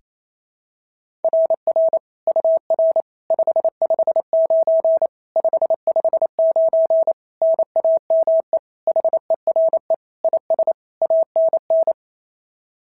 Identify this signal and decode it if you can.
Morse code, heard as a keyed tone, about 25 wpm, 650 Hz. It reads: RR UR 559 559 NAME HERE IS ANN
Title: SOTA QSO Practice 28wpm